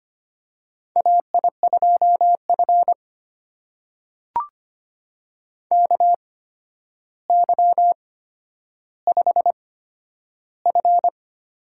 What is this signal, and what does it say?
Morse code, heard as a keyed tone, about 25 wpm, spaced as for 11 wpm, 700 Hz. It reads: AI2F KY5F
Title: USA Call Signs - 4 Characters - 8x Character Spacing - 25wpm